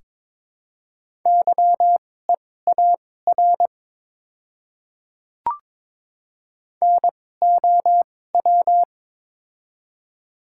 Morse code, heard as a keyed tone, about 22 wpm, spaced as for 16 wpm, 700 Hz. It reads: YEAR NOW
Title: 2x Character Spacing - Top 200 Words - 22wpm